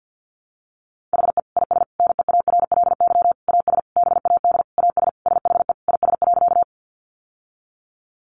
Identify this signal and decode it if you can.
Morse code, heard as a keyed tone, about 50 wpm, 700 Hz. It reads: HE IS NEARLY AS BAD AS SHE IS.
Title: Sentences from Top 1300 Words 50wpm